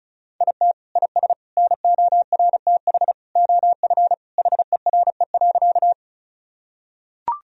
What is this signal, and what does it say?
Morse code, heard as a keyed tone, about 36 wpm, 700 Hz. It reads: IT IS NORTH OF HERE.
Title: Sentences from Top 1000 Words - Review Part 1 - 35wpm